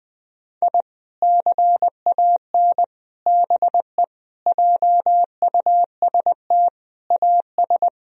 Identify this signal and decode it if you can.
Morse code, heard as a keyed tone, about 21 wpm, 700 Hz. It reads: I CAN BE JUST AS
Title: Sentences from Top 1000 Words - Review Part 3 - 20wpm